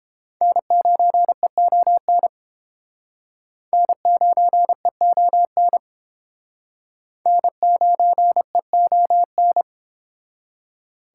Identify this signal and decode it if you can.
Morse code, heard as a keyed tone, about 31 wpm, 700 Hz. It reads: N9EON N9EON N9EON